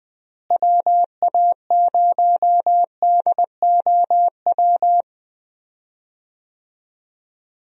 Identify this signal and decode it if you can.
Morse code, heard as a keyed tone, about 21 wpm, 700 Hz. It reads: WA0DOW
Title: USA Call Signs - 6 Characters - Repeated 2x - 20wpm